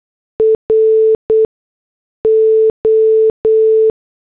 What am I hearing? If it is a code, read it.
Morse code, heard as a keyed tone, about 8 wpm, 440 Hz. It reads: R O
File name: morse / headlines